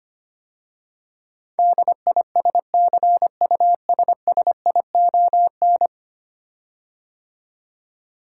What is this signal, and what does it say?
Morse code, heard as a keyed tone, about 25 wpm, 700 Hz. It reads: DISCUSSION
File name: 10 Letter Words - First 3 Letters Given - 25wpm